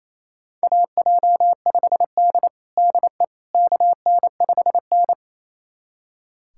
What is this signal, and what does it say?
Morse code, heard as a keyed tone, about 28 wpm, 700 Hz. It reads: AJ5D DE KN5N
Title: USA Callsign DE Callsign - 4 Characters - 28wpm